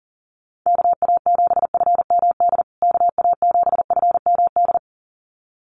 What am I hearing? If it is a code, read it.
Morse code, heard as a keyed tone, about 40 wpm, 700 Hz. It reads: KA7FMD KA7FMD
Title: USA Call Signs Repeated 2x 40wpm